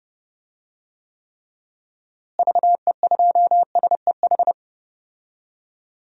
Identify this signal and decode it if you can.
Morse code, heard as a keyed tone, about 31 wpm, 700 Hz. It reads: VE2SEH
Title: DXCC-100 Call Signs - 6 Characters - Repeated 2x - 30wpm